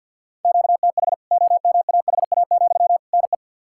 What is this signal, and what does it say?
Morse code, heard as a keyed tone, about 51 wpm, 700 Hz. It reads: QTH OMAHA, NE
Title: K1USN SST QSO Practice 50wpm